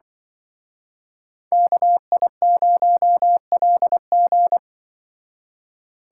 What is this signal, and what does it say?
Morse code, heard as a keyed tone, about 24 wpm, 700 Hz. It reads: KI0LG
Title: Speed Racing - USA 2x2 Call Signs 20wpm